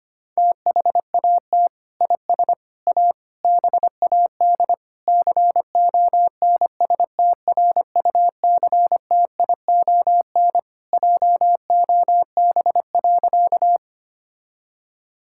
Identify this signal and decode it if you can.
Morse code, heard as a keyed tone, about 25 wpm, 700 Hz. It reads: THAT IS A BAD CONSTRUCTION JOB.
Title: Sentences from Top 1600 Words 25wpm